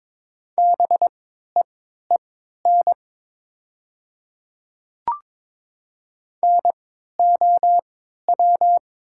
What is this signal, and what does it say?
Morse code, heard as a keyed tone, about 22 wpm, spaced as for 12 wpm, 700 Hz. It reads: BEEN NOW
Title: 3x Character Spacing - Top 100 Words - 22wpm